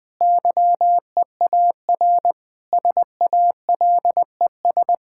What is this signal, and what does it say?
Morse code, heard as a keyed tone, about 20 wpm, 700 Hz. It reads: YEAR SALES
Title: Sentences with 7-Word Phrases - Spoken Beginning - 20wpm